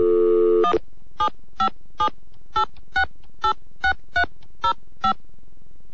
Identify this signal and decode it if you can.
DTMF, heard as telephone touch tones, tones of about 100 ms. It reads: 17578686305